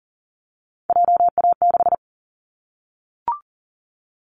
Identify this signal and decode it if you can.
Morse code, heard as a keyed tone, about 40 wpm, 700 Hz. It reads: JA6
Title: Callsign Prefixes for Top 25 DXCC - Repeated 3x - 40wpm